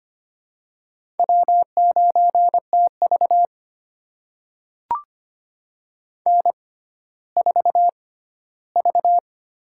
Morse code, heard as a keyed tone, about 25 wpm, spaced as for 14 wpm, 700 Hz. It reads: W9TV N4V
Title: USA Call Signs - 4 Characters - 6x Character Spacing - 25wpm